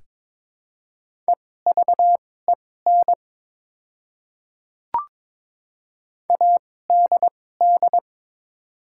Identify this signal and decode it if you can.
Morse code, heard as a keyed tone, about 22 wpm, spaced as for 16 wpm, 690 Hz. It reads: EVEN ADD
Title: 2x Character Spacing - Top 300 Words - 22wpm